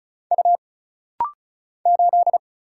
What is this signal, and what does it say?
Morse code, heard as a keyed tone, about 36 wpm, 700 Hz. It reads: U 8
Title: Single Letter-Number Alt - Rapid-Fire 35wpm